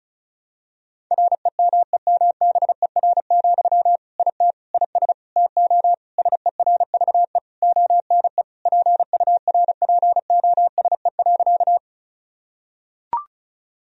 Morse code, heard as a keyed tone, about 36 wpm, 700 Hz. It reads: REMEMBER, IT IS TO SERVE ONE PURPOSE.
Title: Sentences from Top 1000 Words - Review Part 2 - 35wpm